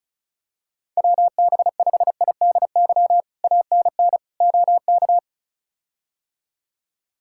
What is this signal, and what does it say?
Morse code, heard as a keyed tone, about 36 wpm, 700 Hz. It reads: WB5IDY ANN OK
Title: K1USN SST QSO Practice - 35wpm